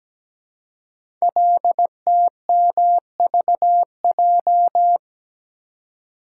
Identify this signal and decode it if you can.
Morse code, heard as a keyed tone, about 17 wpm, 700 Hz. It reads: LTMVJ
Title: Code Groups - 5 Letters - 17wpm